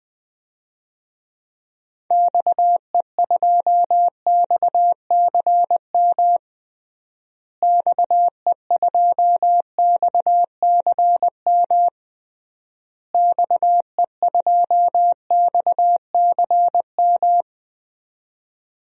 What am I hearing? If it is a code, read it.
Morse code, heard as a keyed tone, about 21 wpm, 700 Hz. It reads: XE2XCM XE2XCM XE2XCM
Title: DXCC-100 Call Signs - 6 Characters - Repeated 3x - 20wpm